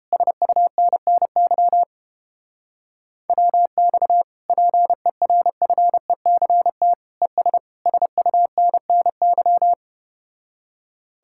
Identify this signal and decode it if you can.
Morse code, heard as a keyed tone, about 30 wpm, 700 Hz. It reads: SUNNY WX PERFECT ES SUNNY